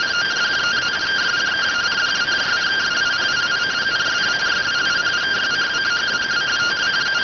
Two-tone teleprinter signal, about 170 Hz shift.